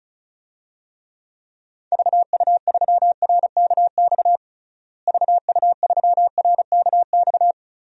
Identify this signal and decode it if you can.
Morse code, heard as a keyed tone, about 34 wpm, 700 Hz. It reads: VU3RKX VU3RKX
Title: DXCC-100 Call Signs - 6 Characters - Repeated 3x - 35wpm